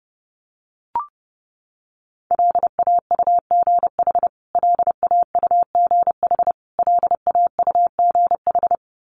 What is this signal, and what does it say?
Morse code, heard as a keyed tone, about 30 wpm, 700 Hz. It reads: LAUGH LAUGH LAUGH
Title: Top 500 Words Repeated 30wpm